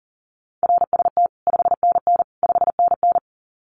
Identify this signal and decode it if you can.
Morse code, heard as a keyed tone, about 40 wpm, 700 Hz. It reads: RST 5NN 5NN
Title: Common QSO Phrases Encore 40wpm